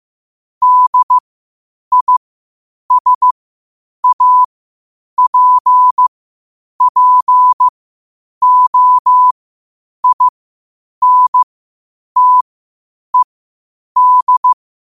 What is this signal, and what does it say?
Morse code, heard as a keyed tone, about 15 wpm, spaced as for 8 wpm, 1000 Hz. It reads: DISAPPOINTED
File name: qotdfiles / 8wpm